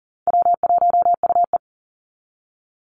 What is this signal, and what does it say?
Morse code, heard as a keyed tone, about 40 wpm, 700 Hz. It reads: W1UE
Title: Easy Call Signs Repeated 3x - 40wpm